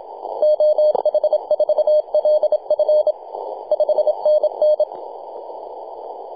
Morse code, heard as a keyed tone, about 27 wpm, 610 Hz. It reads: OH4LF 5NN
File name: t88zo-28-008mhzcw